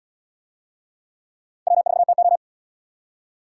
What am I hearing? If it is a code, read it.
Morse code, heard as a keyed tone, about 76 wpm, 700 Hz. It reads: W4TO